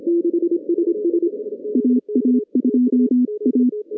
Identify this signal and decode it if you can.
Morse code, heard as a keyed tone, about 26 wpm, 250 Hz. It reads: AA2A
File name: 2015 cq ww cw / 2015 cqwww k5tr - 0-18